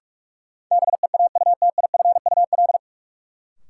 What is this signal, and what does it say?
Morse code, heard as a keyed tone, about 42 wpm, 700 Hz. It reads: BEAUTIFUL
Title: Sentences from Top 900 Words - New Word Drill - 45wpm